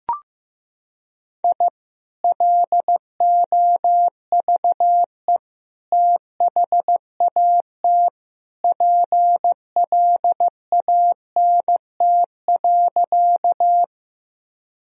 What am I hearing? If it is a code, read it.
Morse code, heard as a keyed tone, about 15 wpm, 700 Hz. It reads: I LOVE THAT PLANT.
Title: Sentences from Top 1000 Words - Review Part 1 - 15wpm